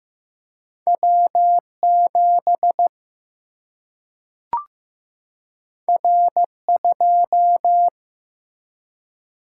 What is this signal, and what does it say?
Morse code, heard as a keyed tone, about 15 wpm, 700 Hz. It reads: W 7 R 2
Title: Single-Letter Single-Number 15wpm